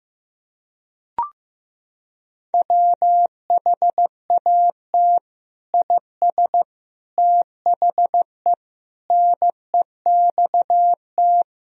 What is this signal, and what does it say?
Morse code, heard as a keyed tone, about 15 wpm, 700 Hz. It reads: WHAT IS THE NEXT
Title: Sentences from Top 1000 Words - Review Part 2 - 15wpm